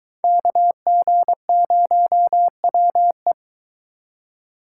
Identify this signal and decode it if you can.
Morse code, heard as a keyed tone, about 22 wpm, 700 Hz. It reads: KG0WE